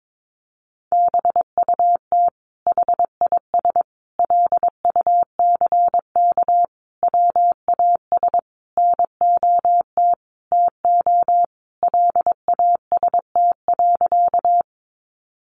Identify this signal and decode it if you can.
Morse code, heard as a keyed tone, about 22 wpm, 700 Hz. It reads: BUT HIS LUCK WAS NOT TO LAST.
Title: Sentences from Top 1500 Words 22wpm